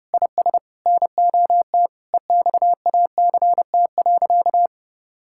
Morse code, heard as a keyed tone, about 30 wpm, 700 Hz. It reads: IS NOT EXACT.
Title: Sentences from Top 1600 Words 30wpm